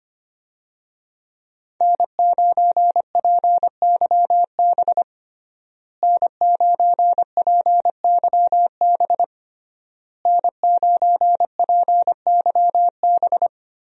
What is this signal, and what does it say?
Morse code, heard as a keyed tone, about 25 wpm, 700 Hz. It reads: N9PYB N9PYB N9PYB
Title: USA Call Signs - 5 Characters - Repeated 3x - 25wpm